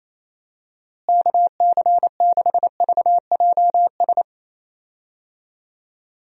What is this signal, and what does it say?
Morse code, heard as a keyed tone, about 27 wpm, 700 Hz. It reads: KC6VJS
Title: USA Call Signs - 6 Characters - Repeated 2x - 28wpm